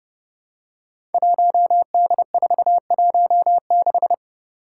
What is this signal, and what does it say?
Morse code, heard as a keyed tone, about 30 wpm, 700 Hz. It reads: 1D416
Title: Five Challenging Letter-Number Combinations 30wpm